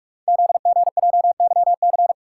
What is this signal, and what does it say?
Morse code, heard as a keyed tone, about 44 wpm, 700 Hz. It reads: ZGJYC
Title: Code Groups - 5 Letter:Numbers - 45wpm